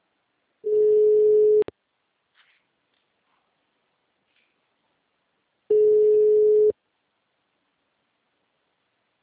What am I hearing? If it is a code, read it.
Ringback tone (Europe, 425 Hz).